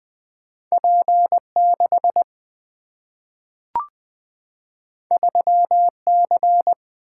Morse code, heard as a keyed tone, about 20 wpm, 700 Hz. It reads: P 6 3 C
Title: Two Letter-Number Combinations - 20wpm